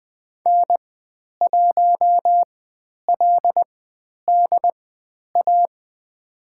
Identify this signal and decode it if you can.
Morse code, heard as a keyed tone, about 20 wpm, spaced as for 10 wpm, 700 Hz. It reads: N1LDA
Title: 109 Lesson 18g - Letter m - Call Signs 20wpm